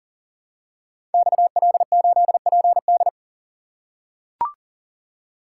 Morse code, heard as a keyed tone, about 42 wpm, 700 Hz. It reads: XL8PD
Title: Code Groups - 5 Letter:Numbers - 40wpm